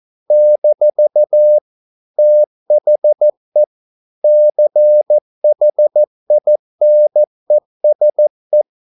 Morse code, heard as a keyed tone, about 14 wpm, 600 Hz. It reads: - THE CHINESE